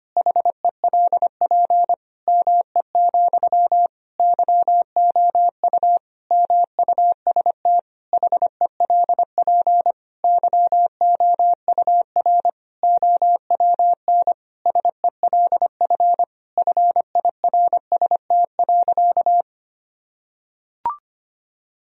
Morse code, heard as a keyed tone, about 25 wpm, 700 Hz. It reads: HELP ME, YOU MUST HELP YOUR OWN SELF FIRST.
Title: Sentences from Top 1000 Words - Review Part 1 - 25wpm